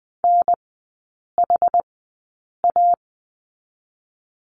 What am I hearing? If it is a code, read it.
Morse code, heard as a keyed tone, about 20 wpm, spaced as for 8 wpm, 700 Hz. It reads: NHA
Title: 060 Lesson 11e - Letter h - Three Characters 20wpm